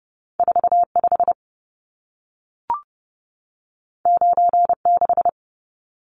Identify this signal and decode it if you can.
Morse code, heard as a keyed tone, about 30 wpm, 700 Hz. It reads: 45 96
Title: Numbers - 2-Digit 30wpm